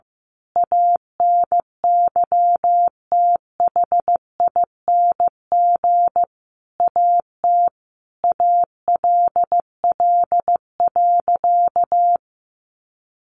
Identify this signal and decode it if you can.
Morse code, heard as a keyed tone, about 15 wpm, 700 Hz. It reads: ANYTHING AT ALL.
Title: Sentences from Top 1300 Words 15wpm